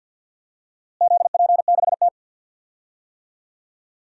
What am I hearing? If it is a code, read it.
Morse code, heard as a keyed tone, about 52 wpm, 700 Hz. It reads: ZP6T
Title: DXCC-100 Call Signs - 4 Characters - Repeated 3x - 50wpm